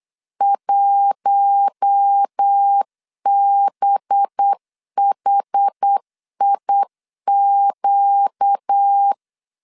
Morse code, heard as a keyed tone, about 8 wpm, 800 Hz. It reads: 1BHIQ
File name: LEZIONI / 6 PKX0 / 6 P 8wpm 3025